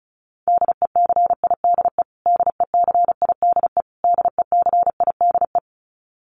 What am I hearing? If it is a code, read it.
Morse code, heard as a keyed tone, about 34 wpm, 700 Hz. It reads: DECIDE DECIDE DECIDE
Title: Top 500 Words Repeated 35wpm